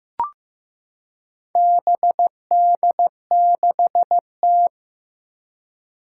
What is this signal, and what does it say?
Morse code, heard as a keyed tone, about 15 wpm, 700 Hz. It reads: BD6T
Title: Four Challenging Letter-Number Combinations 15wpm